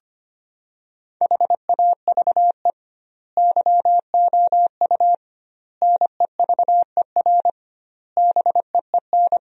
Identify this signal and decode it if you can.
Morse code, heard as a keyed tone, about 25 wpm, 700 Hz. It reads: HAVE YOU NEVER BEEN
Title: 2x Word-Spacing - Sentences from Top 300 Words - 25wpm